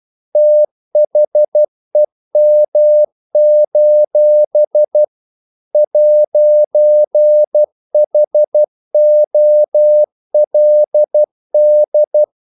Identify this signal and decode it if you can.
Morse code, heard as a keyed tone, about 12 wpm, 600 Hz. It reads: THEM: 'HOLD
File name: CW / Practice / If12